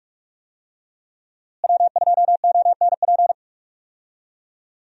Morse code, heard as a keyed tone, about 46 wpm, 700 Hz. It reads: W2ONP
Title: USA Call Signs - 5 Characters - Repeated 3x - 45wpm